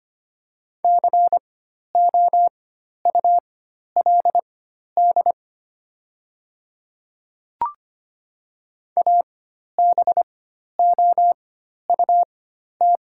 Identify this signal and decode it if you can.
Morse code, heard as a keyed tone, about 25 wpm, spaced as for 12 wpm, 700 Hz. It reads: COULD ABOUT
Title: 4x Character Spacing - Top 100 Words - 25wpm